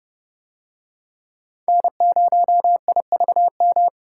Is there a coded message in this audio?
Yes: Morse code from 1.7 to 3.9 s, heard as a keyed tone, about 29 wpm, 700 Hz. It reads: N0IVM